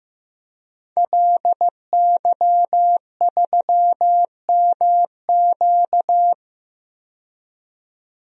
Morse code, heard as a keyed tone, about 15 wpm, 700 Hz. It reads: LY3MQ
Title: DXCC-100 Call Signs - 5 Characters - 15wpm